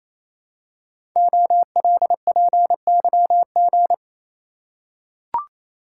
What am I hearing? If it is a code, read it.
Morse code, heard as a keyed tone, about 27 wpm, 700 Hz. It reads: OLPYG
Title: Code Groups - 5 Letters - 28wpm